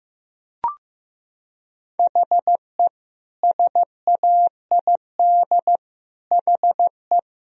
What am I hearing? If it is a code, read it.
Morse code, heard as a keyed tone, about 15 wpm, 700 Hz. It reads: HE SAID HE
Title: Sentences from Top 1000 Words - Review Part 2 - 15wpm